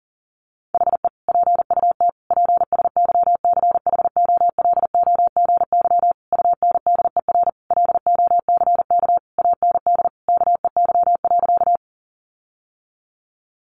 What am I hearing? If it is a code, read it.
Morse code, heard as a keyed tone, about 40 wpm, 700 Hz. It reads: HE PUT PSYCHOLOGY UNDER LOCK AND KEY.
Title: Sentences from Top 1400 Words 40wpm